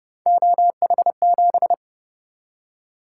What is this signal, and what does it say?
Morse code, heard as a keyed tone, about 29 wpm, 700 Hz. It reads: OH7